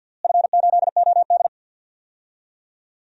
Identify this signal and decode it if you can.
Morse code, heard as a keyed tone, about 49 wpm, 700 Hz. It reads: F8OD
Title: DXCC-100 Call Signs - 4 Characters - Repeated 2x - 50wpm